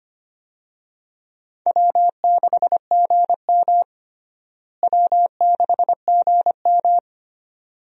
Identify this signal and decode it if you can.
Morse code, heard as a keyed tone, about 25 wpm, 700 Hz. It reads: W6GM W6GM
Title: USA Call Signs - 4 Characters - Repeated 2x - 25wpm